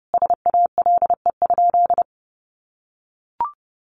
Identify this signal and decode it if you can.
Morse code, heard as a keyed tone, about 30 wpm, 700 Hz. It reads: SALE?
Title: Sentences from Top 1400 Words 30wpm